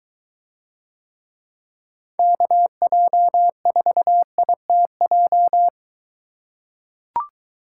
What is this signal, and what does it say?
Morse code, heard as a keyed tone, about 22 wpm, 700 Hz. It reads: KJ4ITJ